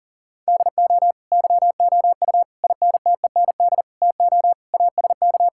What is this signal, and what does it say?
Morse code, heard as a keyed tone, about 37 wpm, 700 Hz. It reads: DO YOU INTEND TO ASK
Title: Sentences from Top 1600 Words 40wpm